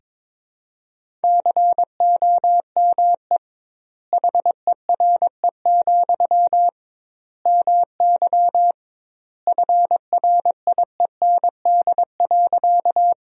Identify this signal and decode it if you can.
Morse code, heard as a keyed tone, about 21 wpm, 700 Hz. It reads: COME HERE, MY FRIEND.
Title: 2x Word-Spacing - Sentences from Top 300 Words - 22wpm